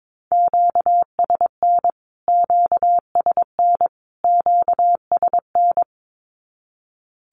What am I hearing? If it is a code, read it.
Morse code, heard as a keyed tone, about 22 wpm, 700 Hz. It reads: QSN QSN QSN
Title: Q-Codes Repeated 3x 22wpm